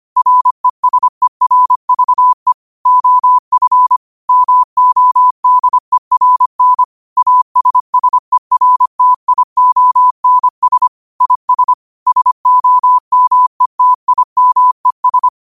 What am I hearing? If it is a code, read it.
Morse code, heard as a keyed tone, about 25 wpm, 1000 Hz. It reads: RESERVE OF MODERN ASSERTIONS IS SOMETIMES
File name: qotdfiles / 25wpm